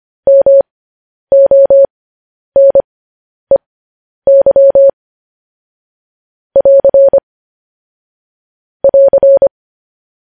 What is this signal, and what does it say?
Morse code, heard as a keyed tone, about 25 wpm, spaced as for 10 wpm, 550 Hz. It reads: MONEY <AR> <AR>
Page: News Headlines in Morse Code at 10 WPM